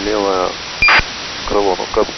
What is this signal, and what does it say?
ACARS aircraft data, with speech.
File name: uploader / 29.09.2018 afl1946